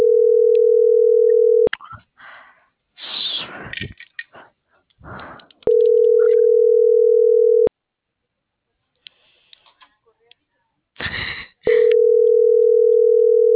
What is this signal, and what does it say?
Ringback tone (North America), 440 + 480 Hz.
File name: CENTRAL BLN / out ximena.diaz - .63338